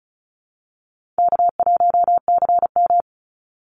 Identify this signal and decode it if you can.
Morse code, heard as a keyed tone, about 34 wpm, 700 Hz. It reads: K1CM
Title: NA SOTA Chaser Callsigns 35wpm